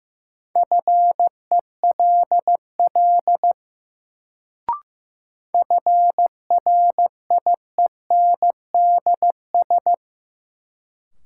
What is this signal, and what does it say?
Morse code, heard as a keyed tone, about 15 wpm, 700 Hz. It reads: FELL FRIENDS
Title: Sentences from Top 1600 Words - New Word Drill Alt - 15wpm